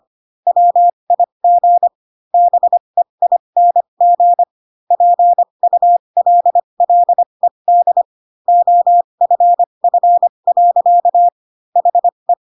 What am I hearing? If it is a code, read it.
Morse code, heard as a keyed tone, about 25 wpm, spaced as for 22 wpm, 700 Hz. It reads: WIG BEING PULLED OFF. HE
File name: Pinocchio / Pinocchio ch3 18 25 ik1hge